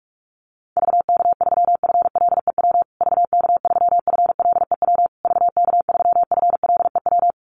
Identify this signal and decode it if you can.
Morse code, heard as a keyed tone, about 44 wpm, 700 Hz. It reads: VK3FLEW VK3FLEW VK3FLEW
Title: USA Call Signs Repeated 3x 45wpm